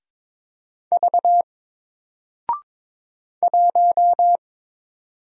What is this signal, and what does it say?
Morse code, heard as a keyed tone, about 23 wpm, 700 Hz. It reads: V 1
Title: Single Letter-Number Alt 22wpm